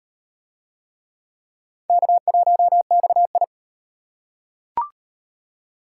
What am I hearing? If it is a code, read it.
Morse code, heard as a keyed tone, about 38 wpm, 700 Hz. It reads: K1XI